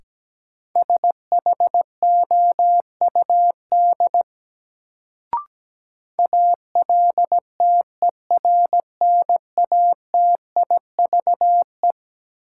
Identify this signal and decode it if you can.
Morse code, heard as a keyed tone, about 17 wpm, 700 Hz. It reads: SHOUD ALTERNATIVE
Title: Sentences from Top 1000 Words - Review Part 4 - New Word Drill - 17wpm